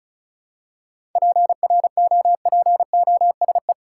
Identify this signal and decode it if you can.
Morse code, heard as a keyed tone, about 34 wpm, 700 Hz. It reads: PROPOSE
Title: Seven Letter Words - 35wpm